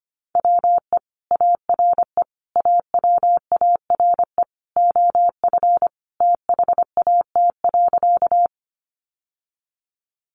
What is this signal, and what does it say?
Morse code, heard as a keyed tone, about 25 wpm, 700 Hz. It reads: WE ARE AWARE OF THAT.
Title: Sentences from Top 1200 Words 25wpm